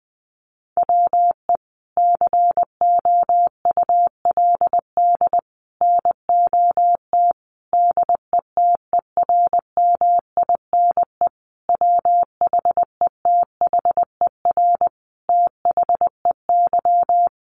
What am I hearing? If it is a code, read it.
Morse code, heard as a keyed tone, about 20 wpm, 700 Hz. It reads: WE COULD NOT DETERMINE WHETHER THEY
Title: Sentences from Top 1300 Words 20wpm